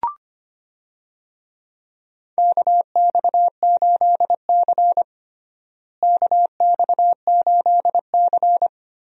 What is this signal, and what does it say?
Morse code, heard as a keyed tone, about 25 wpm, 700 Hz. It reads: KX8C KX8C
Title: USA Call Signs - 4 Characters - Repeated 2x - 25wpm